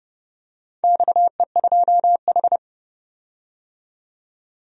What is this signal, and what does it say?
Morse code, heard as a keyed tone, about 30 wpm, 700 Hz. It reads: XE2H
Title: DXCC-100 Call Signs - 4 Characters - 30wpm